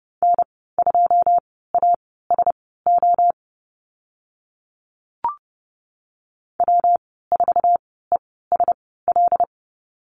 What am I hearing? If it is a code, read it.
Morse code, heard as a keyed tone, about 30 wpm, spaced as for 17 wpm, 700 Hz. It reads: N2ASO W4ESL
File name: Lesson 19g - Letter w - Call Signs 30wpm